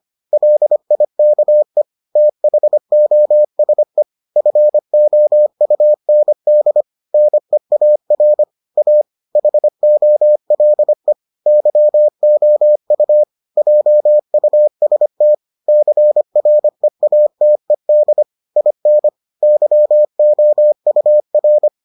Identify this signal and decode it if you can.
Morse code, heard as a keyed tone, about 25 wpm, 600 Hz. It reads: LIKE THOSE FOUND NEAR A HOLE YOU JUST CREATED IN YOUR